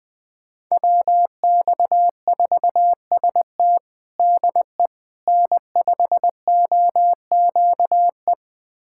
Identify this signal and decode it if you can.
Morse code, heard as a keyed tone, about 20 wpm, 700 Hz. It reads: WX4ST DE N5OQE
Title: USA Callsign DE Callsign 20wpm